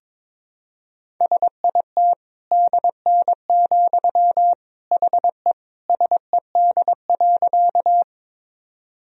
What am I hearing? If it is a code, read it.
Morse code, heard as a keyed tone, about 22 wpm, 700 Hz. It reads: SIT DN, HE SED.
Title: Sentences from Top 500 Words and Abbreviations 22wpm